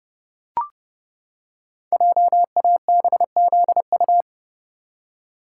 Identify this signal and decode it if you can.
Morse code, heard as a keyed tone, about 31 wpm, 700 Hz. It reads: JABZU
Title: Code Groups - 5 Letters - 30wpm